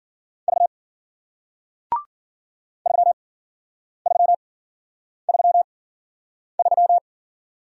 Morse code, heard as a keyed tone, about 53 wpm, spaced as for 9 wpm, 700 Hz. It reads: V 3333